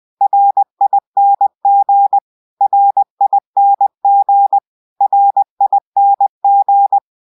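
Morse code, heard as a keyed tone, about 20 wpm, 800 Hz. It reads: RING RING RING
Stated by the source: Morse code spelling ringtone.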